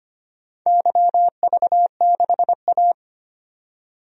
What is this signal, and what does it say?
Morse code, heard as a keyed tone, about 25 wpm, 700 Hz. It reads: YV6A
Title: DXCC-100 Call Signs - 4 Characters - 25wpm